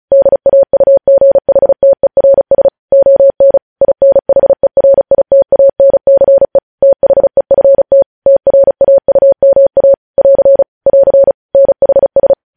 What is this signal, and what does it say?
Morse code, heard as a keyed tone, about 36 wpm, 550 Hz. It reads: DAUGHTERS ON INHERITANCE THEFT TRAUMA <AR> <AR> NHS
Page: news headlines in morse code at 35 wpm